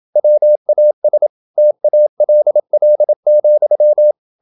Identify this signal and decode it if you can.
Morse code, heard as a keyed tone, about 27 wpm, 600 Hz. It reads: WAS TALL,
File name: Never 0027wpm